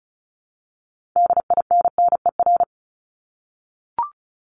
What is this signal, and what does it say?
Morse code, heard as a keyed tone, about 34 wpm, 700 Hz. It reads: DINNER
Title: Top 1751-2000 Words 35wpm